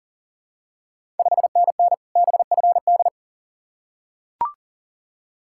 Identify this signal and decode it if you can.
Morse code, heard as a keyed tone, about 38 wpm, 700 Hz. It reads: 5NN BFD
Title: Ontario QSO Party Exchange 40wpm